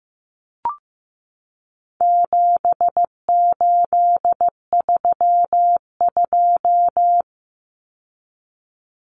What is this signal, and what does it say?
Morse code, heard as a keyed tone, about 15 wpm, 700 Hz. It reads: 7832
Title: Common Cut Numbers - 4-Digit 15wpm